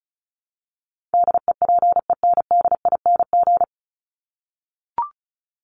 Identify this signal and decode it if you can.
Morse code, heard as a keyed tone, about 34 wpm, 700 Hz. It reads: DEPENDING
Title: ING Suffix - Top 100 Words with Base Words 35wpm